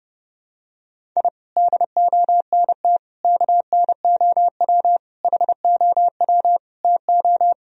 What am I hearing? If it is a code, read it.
Morse code, heard as a keyed tone, about 30 wpm, 700 Hz. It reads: I DONT KNOW HOW TO
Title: Sentences from Top 1000 Words - Review Part 1 - 30wpm